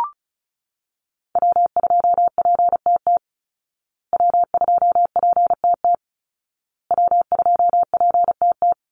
Morse code, heard as a keyed tone, about 34 wpm, 700 Hz. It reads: W2PTT W2PTT W2PTT
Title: Call Signs Repeated 2x - 35wpm